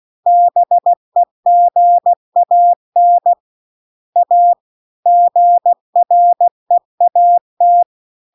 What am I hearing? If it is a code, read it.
Morse code, heard as a keyed tone, about 16 wpm, 700 Hz. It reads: BEGAN A GREAT
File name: CW-The-Big-Twoheaded-River-2-16WPM